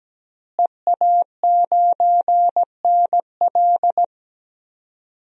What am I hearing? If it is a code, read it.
Morse code, heard as a keyed tone, about 17 wpm, 700 Hz. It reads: EA9NL
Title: DXCC-100 Call Signs - 5 Characters - Repeated 3x - 17wpm